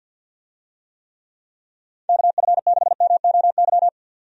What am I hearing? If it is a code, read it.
Morse code, heard as a keyed tone, about 47 wpm, 700 Hz. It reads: KF6MOY
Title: USA Call Signs - 6 Characters - 50wpm